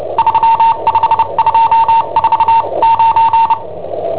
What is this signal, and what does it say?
Morse code, heard as a keyed tone, about 29 wpm, spaced as for 25 wpm, 910 Hz. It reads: 35249
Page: Cassetta esercizi preparata da I7OHP per IW5ELP